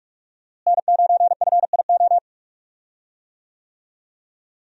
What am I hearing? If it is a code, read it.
Morse code, heard as a keyed tone, about 41 wpm, 700 Hz. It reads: N9FIO
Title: USA Call Signs - 5 Characters - Repeated 3x - 45wpm